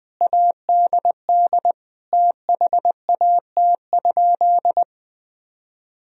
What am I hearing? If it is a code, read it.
Morse code, heard as a keyed tone, about 20 wpm, 700 Hz. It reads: ADD THAT?
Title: Sentences from Top 1000 Words - Review Part 1 - 20wpm